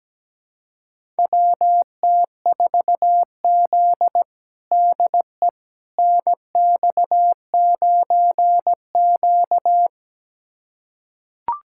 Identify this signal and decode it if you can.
Morse code, heard as a keyed tone, about 17 wpm, 700 Hz. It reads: WT4Z DE NX9Q
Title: USA Callsign DE Callsign - 4 Characters - 17wpm